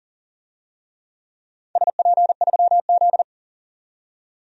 Morse code, heard as a keyed tone, about 40 wpm, 700 Hz. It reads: SP3Z
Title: DXCC-100 Call Signs - 4 Characters - 40wpm